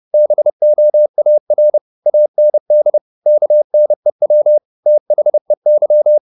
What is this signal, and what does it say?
Morse code, heard as a keyed tone, about 30 wpm, 600 Hz. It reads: BOAR AND KNEW THEY
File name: the old man and the sea / the old man and the sea 1 30wpm